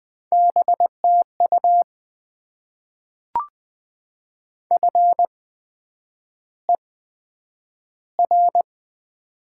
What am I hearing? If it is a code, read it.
Morse code, heard as a keyed tone, about 20 wpm, spaced as for 9 wpm, 700 Hz. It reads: BTU FER
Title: Top 20 QSO Elements - 8x Character Spacing - 20wpm